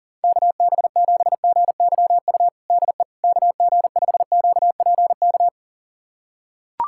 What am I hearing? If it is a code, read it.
Morse code, heard as a keyed tone, about 37 wpm, 700 Hz. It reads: KB7GYU DE KG5QPK
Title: USA Callsign DE Callsign - 6 Characters - 40wpm